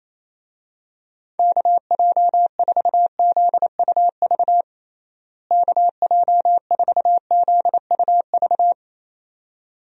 Morse code, heard as a keyed tone, about 28 wpm, 700 Hz. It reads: KJ4ZUV KJ4ZUV
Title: USA Call Signs - 6 Characters - Repeated 2x - 28wpm